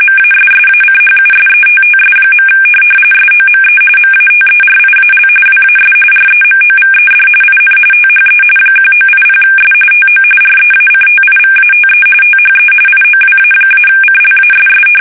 RTTY, about 45.45 baud. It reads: MASSACHUSETTS. SOME MATERIAL WAS REMOVED IN THE 1980S TO REACTIVATE THE IOWA-CLASS
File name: fsk850 / USS-Massachusetts